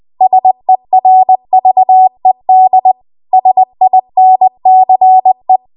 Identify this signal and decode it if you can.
Morse code, heard as a keyed tone, about 20 wpm, 750 Hz. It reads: SERVED SINCE
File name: Archive / 20 WPM / 20WPM